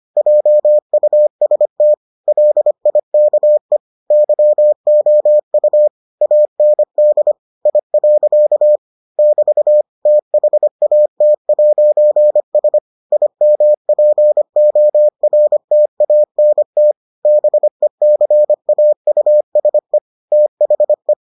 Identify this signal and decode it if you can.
Morse code, heard as a keyed tone, about 25 wpm, 600 Hz. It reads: JUST LIKE YOU AND I. <BT> THAT'S IMPORTANT BECAUSE THE